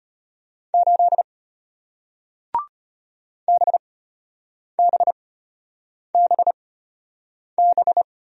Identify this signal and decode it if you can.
Morse code, heard as a keyed tone, about 33 wpm, spaced as for 8 wpm, 700 Hz. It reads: 8 BBBB